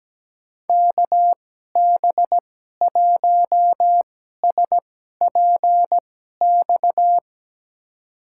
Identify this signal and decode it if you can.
Morse code, heard as a keyed tone, about 17 wpm, spaced as for 12 wpm, 700 Hz. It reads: KB1SPX
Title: USA Call Signs - 6 Characters - 2x Character Spacing - 17wpm